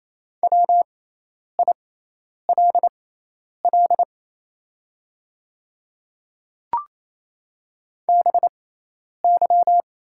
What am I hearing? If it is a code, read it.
Morse code, heard as a keyed tone, about 28 wpm, spaced as for 10 wpm, 690 Hz. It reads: WILL BY
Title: 6x Character Spacing - Top 200 Words - 28wpm